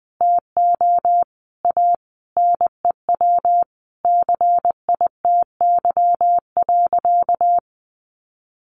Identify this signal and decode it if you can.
Morse code, heard as a keyed tone, about 20 wpm, 700 Hz. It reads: TO A NEW CITY.
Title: Easy Phrases 20wpm